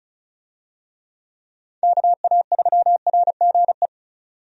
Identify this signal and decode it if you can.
Morse code, heard as a keyed tone, about 36 wpm, 700 Hz. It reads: KA3RGE
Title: USA Call Signs - 6 Characters - 35wpm